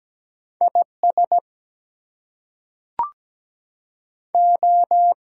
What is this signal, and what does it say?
Morse code, heard as a keyed tone, about 17 wpm, 700 Hz. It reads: IS O
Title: Two Letter Words - 17wpm